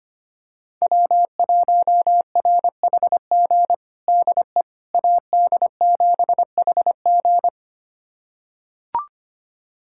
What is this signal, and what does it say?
Morse code, heard as a keyed tone, about 25 wpm, 700 Hz. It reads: W1RHG DE AD7HG
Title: USA Callsign DE Callsign - 5 Characters - 25wpm